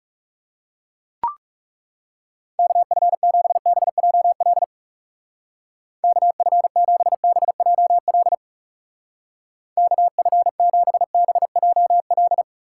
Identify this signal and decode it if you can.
Morse code, heard as a keyed tone, about 36 wpm, 700 Hz. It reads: KF7BJL KF7BJL KF7BJL
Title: Speed Racing - USA 2x3 Call Signs 30wpm